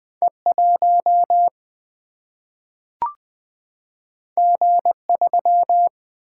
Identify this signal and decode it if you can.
Morse code, heard as a keyed tone, about 20 wpm, 700 Hz. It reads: E1 G3
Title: Single-Letter Single-Number 20wpm